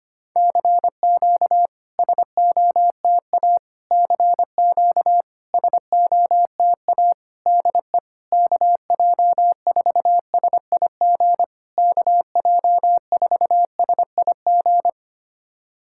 Morse code, heard as a keyed tone, about 25 wpm, 700 Hz. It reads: CQ SOTA CQ SOTA DE KJ4SIG KJ4SIG
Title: SOTA QSO Practice 25wpm